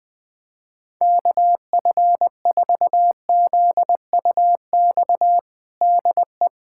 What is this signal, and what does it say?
Morse code, heard as a keyed tone, about 21 wpm, 700 Hz. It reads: KF4ZUX DE
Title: USA Callsign DE Callsign - 6 Characters - 20wpm